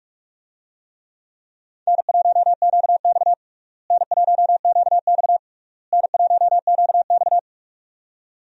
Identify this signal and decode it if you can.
Morse code, heard as a keyed tone, about 41 wpm, 700 Hz. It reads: N1QX N1QX N1QX
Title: USA Call Signs - 4 Characters - Repeated 3x - 45wpm